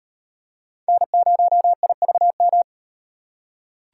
Morse code, heard as a keyed tone, about 38 wpm, 700 Hz. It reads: N0IVM